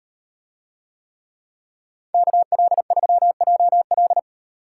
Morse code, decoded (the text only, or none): KL3JL